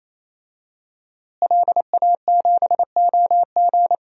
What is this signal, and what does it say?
Morse code, heard as a keyed tone, about 28 wpm, 700 Hz. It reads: LA7OG
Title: DXCC-100 Call Signs - 5 Characters - Repeated 3x - 28wpm